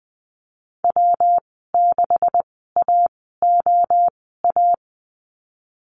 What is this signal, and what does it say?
Morse code, heard as a keyed tone, about 20 wpm, spaced as for 15 wpm, 700 Hz. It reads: W6AOA
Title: 140 Lesson 23g - Letter f - Call Signs 20wpm